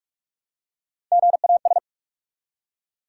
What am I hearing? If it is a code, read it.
Morse code, heard as a keyed tone, about 42 wpm, 700 Hz. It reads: GAS
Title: Three Letter Words - 45wpm